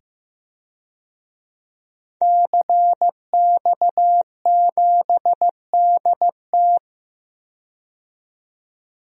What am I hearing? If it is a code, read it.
Morse code, heard as a keyed tone, about 15 wpm, 700 Hz. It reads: CX7DT
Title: DXCC-100 Call Signs - 5 Characters - 15wpm